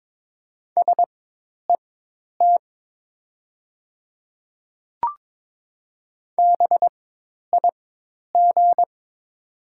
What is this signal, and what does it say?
Morse code, heard as a keyed tone, about 22 wpm, spaced as for 10 wpm, 700 Hz. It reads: SET BIG
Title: 4x Character Spacing - Top 200 Words - 22wpm